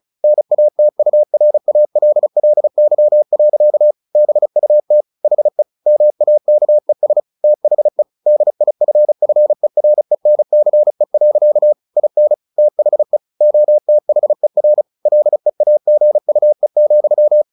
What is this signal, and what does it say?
Morse code, heard as a keyed tone, about 34 wpm, 600 Hz. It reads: NATURALLY. BUT HE MAKES THE DIFFERENCE. IN THE OTHER LEAGUE,
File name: the old man and the sea / the old man and the sea 1 35wpm